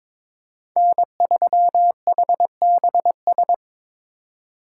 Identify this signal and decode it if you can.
Morse code, heard as a keyed tone, about 25 wpm, 700 Hz. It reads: N3HBS
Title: Speed Racing - USA 1x3 Call Signs 22wpm